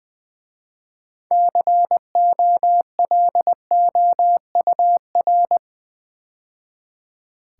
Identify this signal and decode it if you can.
Morse code, heard as a keyed tone, about 20 wpm, 700 Hz. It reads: COLOUR
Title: 6 Letter Words - First 3 Letters Given - 20wpm